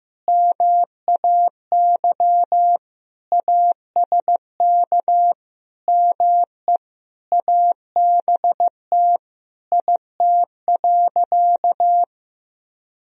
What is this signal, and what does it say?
Morse code, heard as a keyed tone, about 15 wpm, 700 Hz. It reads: MAY ASK ME ABT IT.
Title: Sentences from Top 200 Words and Abbreviations 15wpm